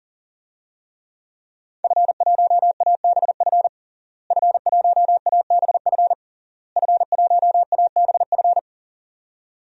Morse code, heard as a keyed tone, about 40 wpm, 700 Hz. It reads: F1ABF F1ABF F1ABF
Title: DXCC-100 Call Signs - 5 Characters - Repeated 3x - 40wpm